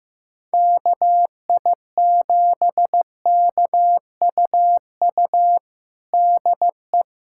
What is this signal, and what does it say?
Morse code, heard as a keyed tone, about 15 wpm, 700 Hz. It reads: KI7KUU DE
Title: USA Callsign DE Callsign 15wpm